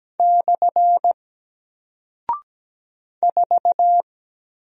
Morse code, heard as a keyed tone, about 17 wpm, 700 Hz. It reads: / 4
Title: Single Character (Letter:Number:Punctuation:Prosign) Alt 17wpm